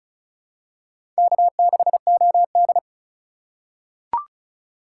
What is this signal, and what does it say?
Morse code, heard as a keyed tone, about 36 wpm, 700 Hz. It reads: K6OD
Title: USA Call Signs - 4x Character Spacing - 35wpm